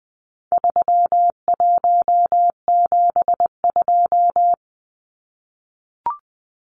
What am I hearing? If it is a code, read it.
Morse code, heard as a keyed tone, about 20 wpm, 700 Hz. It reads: 3172
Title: Numbers - 4-Digit 20wpm